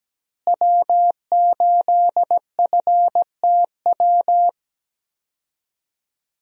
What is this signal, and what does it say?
Morse code, heard as a keyed tone, about 17 wpm, 700 Hz. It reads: W8FTW
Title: USA Call Signs - 5 Characters - Repeated 2x - 17wpm